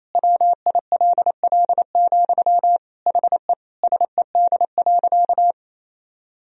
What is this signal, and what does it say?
Morse code, heard as a keyed tone, about 28 wpm, 700 Hz. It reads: WILL, HE SED.
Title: Sentences from Top 200 Words and Abbreviations 28wpm